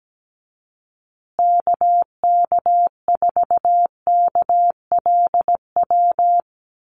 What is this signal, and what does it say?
Morse code, heard as a keyed tone, about 17 wpm, 700 Hz. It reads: KK4KLW
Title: USA Call Signs Repeated 3x 17wpm